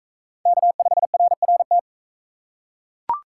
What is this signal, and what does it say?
Morse code, heard as a keyed tone, about 39 wpm, 700 Hz. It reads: K5RRT